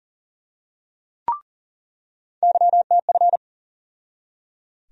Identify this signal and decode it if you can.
Morse code, heard as a keyed tone, about 38 wpm, 700 Hz. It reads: YTF
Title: USA Call Sign Postfixes - 3 Letters 40wpm